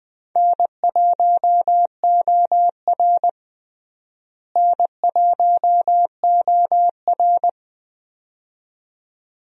Morse code, heard as a keyed tone, about 20 wpm, 700 Hz. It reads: N1OR N1OR
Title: USA Call Signs - 4 Characters - Repeated 2x - 20wpm